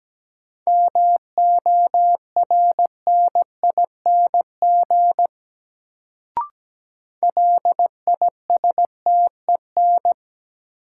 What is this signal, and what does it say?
Morse code, heard as a keyed tone, about 17 wpm, 700 Hz. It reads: MORNING LISTEN
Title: Sentences from Top 400 Words - New Word Drill - 17wpm